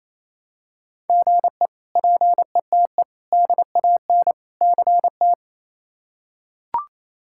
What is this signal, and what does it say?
Morse code, heard as a keyed tone, about 27 wpm, 700 Hz. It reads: GE PETE DAN CT
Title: K1USN SST QSO Practice - 28wpm